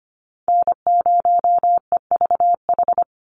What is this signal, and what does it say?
Morse code, heard as a keyed tone, about 25 wpm, 700 Hz. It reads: N0EVH
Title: NA SOTA Chaser Callsigns 25wpm